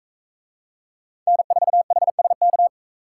Morse code, heard as a keyed tone, about 43 wpm, 700 Hz. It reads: N4HSK